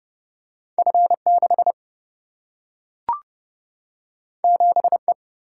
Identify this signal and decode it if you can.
Morse code, heard as a keyed tone, about 29 wpm, 700 Hz. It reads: F6 7E
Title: Single-Letter Single-Number 30wpm